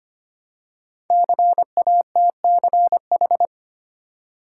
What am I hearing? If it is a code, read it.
Morse code, heard as a keyed tone, about 25 wpm, 700 Hz. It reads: CATCH
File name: 5 Letter Words - First 3 Letters Given - 25wpm